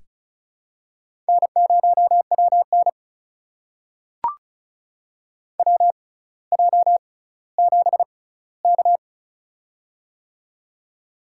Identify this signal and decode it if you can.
Morse code, heard as a keyed tone, about 36 wpm, spaced as for 18 wpm, 700 Hz. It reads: N0WN WJ7K
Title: USA Call Signs - 4 Characters - 6x Character Spacing - 35wpm